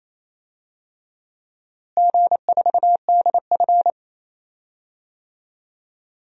Morse code, heard as a keyed tone, about 28 wpm, 700 Hz. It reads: G4DF
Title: DXCC-100 Call Signs - 4 Characters - Repeated 3x - 28wpm